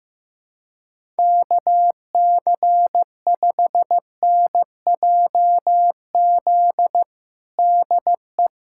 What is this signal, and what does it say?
Morse code, heard as a keyed tone, about 15 wpm, 700 Hz. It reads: KC5NJZ DE
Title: USA Callsign DE Callsign 15wpm